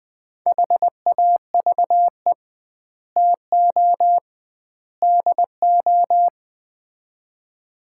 Morse code, heard as a keyed tone, about 20 wpm, 700 Hz. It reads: HAVE TO DO
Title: 2x Word-Spacing - Sets of 3 Words - 20wpm